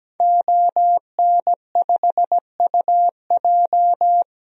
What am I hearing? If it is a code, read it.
Morse code, heard as a keyed tone, about 17 wpm, 700 Hz. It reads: ON5UJ
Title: DXCC-100 Call Signs - 5 Characters - Repeated 2x - 17wpm